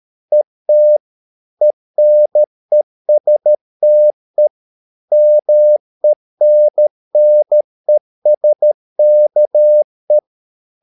Morse code, heard as a keyed tone, about 13 wpm, 600 Hz. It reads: ET ENESTE MENNESKE
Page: Ferie 13 wpm